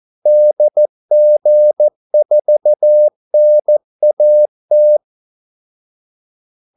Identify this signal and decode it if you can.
Morse code, heard as a keyed tone, about 14 wpm, 600 Hz. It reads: DG4NAT
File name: Callsigns-14wpm